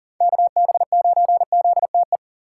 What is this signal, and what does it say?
Morse code, heard as a keyed tone, about 51 wpm, 700 Hz. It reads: KB9ZTE